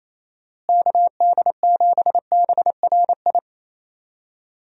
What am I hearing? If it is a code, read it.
Morse code, heard as a keyed tone, about 28 wpm, 700 Hz. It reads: KD7BRI
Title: USA Call Signs - 6 Characters - Repeated 3x - 28wpm